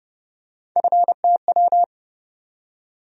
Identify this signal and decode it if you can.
Morse code, heard as a keyed tone, about 30 wpm, 700 Hz. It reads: FTW
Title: USA Call Sign Postfixes - 3 Letters 30wpm